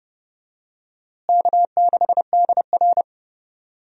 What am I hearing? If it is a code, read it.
Morse code, heard as a keyed tone, about 30 wpm, 700 Hz. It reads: K6DR
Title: USA Call Signs - 4 Characters - Repeated 2x - 30wpm